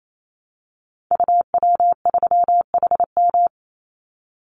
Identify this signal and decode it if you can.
Morse code, heard as a keyed tone, about 28 wpm, 700 Hz. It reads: UW3HM
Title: Call Signs - 28wpm